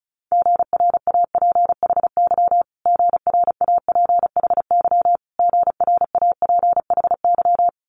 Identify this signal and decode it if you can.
Morse code, heard as a keyed tone, about 36 wpm, 700 Hz. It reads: GRAPHY GRAPHY GRAPHY
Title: Multi-letter Phonemes Repeated 3x 35wpm